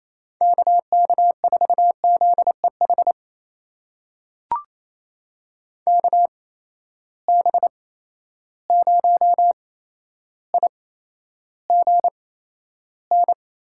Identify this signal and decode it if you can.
Morse code, heard as a keyed tone, about 28 wpm, spaced as for 12 wpm, 700 Hz. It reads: KK4ZEH KB0IGN
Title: USA Call Signs - 6 Characters - 8x Character Spacing - 28wpm